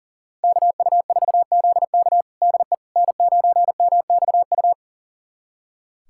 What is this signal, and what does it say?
Morse code, heard as a keyed tone, about 37 wpm, 690 Hz. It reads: KU4ZK DE N9MXU
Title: USA Callsign DE Callsign - 5 Characters - 40wpm